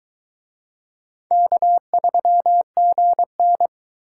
Morse code, heard as a keyed tone, about 21 wpm, 700 Hz. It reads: K3GN